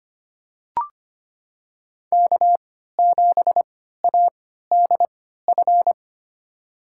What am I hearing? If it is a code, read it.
Morse code, heard as a keyed tone, about 25 wpm, spaced as for 14 wpm, 700 Hz. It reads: K7ADF
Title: USA Call Signs - 5 Characters - 3x Character Spacing - 25wpm